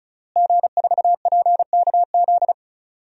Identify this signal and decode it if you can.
Morse code, heard as a keyed tone, about 36 wpm, 700 Hz. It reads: G4PKZ
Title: DXCC-100 Call Signs - 5 Characters - Repeated 3x - 35wpm